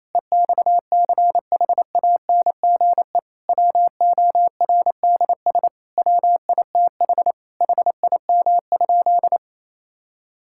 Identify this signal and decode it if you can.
Morse code, heard as a keyed tone, about 27 wpm, 700 Hz. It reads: EXCHANGE WORDS WITH HIM?
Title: Sentences from Top 1000 Words - Review Part 6 - 28wpm